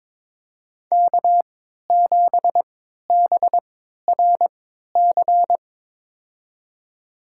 Morse code, heard as a keyed tone, about 22 wpm, spaced as for 13 wpm, 700 Hz. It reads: K7BRC
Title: USA Call Signs - 5 Characters - 3x Character Spacing - 22wpm